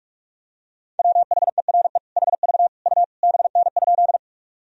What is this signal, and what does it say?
Morse code, heard as a keyed tone, about 46 wpm, 700 Hz. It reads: WHERE HV U BN?
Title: Sentences from Top 200 Words and Abbreviations 45wpm